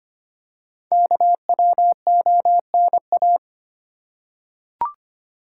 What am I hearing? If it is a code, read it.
Morse code, heard as a keyed tone, about 25 wpm, 700 Hz. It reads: KWONA
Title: Code Groups - 5 Letters - 25wpm